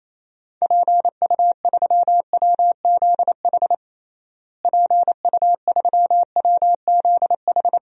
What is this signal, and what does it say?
Morse code, heard as a keyed tone, about 28 wpm, 700 Hz. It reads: PU3WZH PU3WZH
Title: DXCC-100 Call Signs - 6 Characters - Repeated 2x - 28wpm